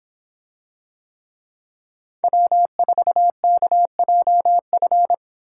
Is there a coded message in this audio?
Yes: Morse code from 2.2 to 5.1 s, heard as a keyed tone, about 25 wpm, 700 Hz. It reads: W4KJF